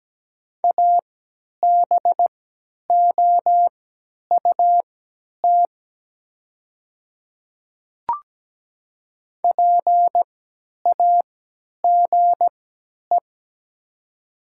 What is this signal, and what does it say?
Morse code, heard as a keyed tone, about 17 wpm, spaced as for 10 wpm, 700 Hz. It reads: ABOUT PAGE
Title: 3x Character Spacing - Top 300 Words - 17wpm